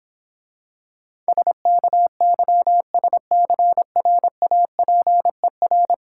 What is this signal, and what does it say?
Morse code, heard as a keyed tone, about 26 wpm, 700 Hz. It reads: SKYSCRAPER